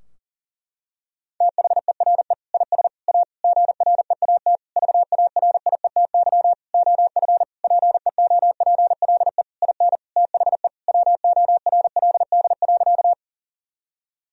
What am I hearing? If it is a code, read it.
Morse code, heard as a keyed tone, about 40 wpm, 700 Hz. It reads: THERE IS A GREAT VARIETY OF PEOPLE IN THE WORLD.
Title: Sentences from Top 1000 Words - Review Part 4 - 40wpm